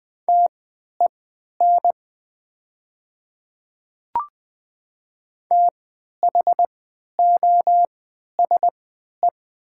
Morse code, heard as a keyed tone, about 20 wpm, spaced as for 11 wpm, 690 Hz. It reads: TEN THOSE
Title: 3x Character Spacing - Top 500 Words - 20wpm